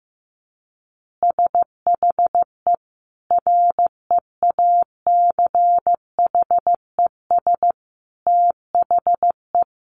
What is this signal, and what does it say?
Morse code, heard as a keyed tone, about 14 wpm, 700 Hz. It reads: SHE REACHES THE
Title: Easy Phrases 15wpm